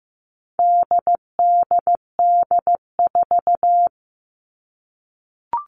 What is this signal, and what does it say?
Morse code, heard as a keyed tone, about 15 wpm, 700 Hz. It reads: DDD4
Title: All Cut Numbers - 4-Digit 15wpm